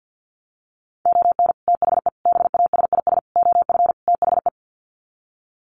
Morse code, heard as a keyed tone, about 50 wpm, 700 Hz. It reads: ON THE BASIS OF THE
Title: Sets of 5 Words 50wpm